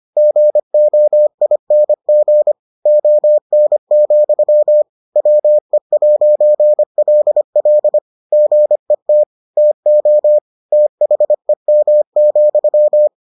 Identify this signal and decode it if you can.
Morse code, heard as a keyed tone, about 25 wpm, 600 Hz. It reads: GOING ON, WE'LL GET TO THEM,